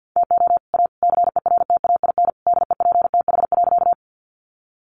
Morse code, heard as a keyed tone, about 50 wpm, 700 Hz. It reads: TO A CERTAIN DEPTH.
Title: Sentences from Top 1200 Words 50wpm